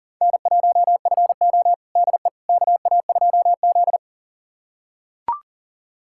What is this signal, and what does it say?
Morse code, heard as a keyed tone, about 38 wpm, 700 Hz. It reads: N1FO DE KA2Z
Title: USA Callsign DE Callsign - 4 Characters - 40wpm